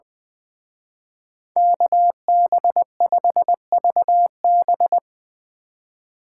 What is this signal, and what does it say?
Morse code, heard as a keyed tone, about 22 wpm, 700 Hz. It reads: KB5VB
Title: Speed Racing - USA 2x2 Call Signs 20wpm